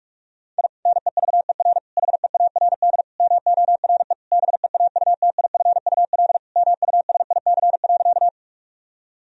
Morse code, heard as a keyed tone, about 46 wpm, 700 Hz. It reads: I NEVER HEARD MORE BEAUTIFUL MUSIC.
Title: Sentences from Top 1000 Words - Review Part 1 - 45wpm